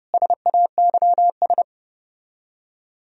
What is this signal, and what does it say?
Morse code, heard as a keyed tone, about 34 wpm, 700 Hz. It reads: SAYS